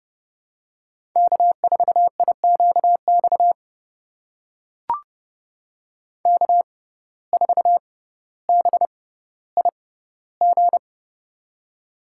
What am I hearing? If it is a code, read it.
Morse code, heard as a keyed tone, about 30 wpm, spaced as for 15 wpm, 700 Hz. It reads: K4IQX K4BIG
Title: USA Call Signs - 5 Characters - 6x Character Spacing - 30wpm